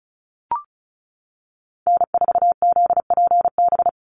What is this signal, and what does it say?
Morse code, heard as a keyed tone, about 36 wpm, 700 Hz. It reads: N4ZPB
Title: USA Call Signs 35wpm